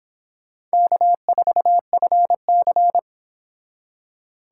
Morse code, heard as a keyed tone, about 26 wpm, 700 Hz. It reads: K4FC